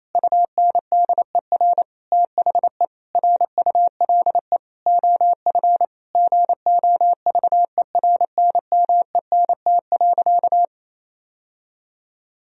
Morse code, heard as a keyed tone, about 28 wpm, 700 Hz. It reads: UNDER THE RULE OF GOVERNMENT.
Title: Sentences from Top 1000 Words - Review Part 3 - 28wpm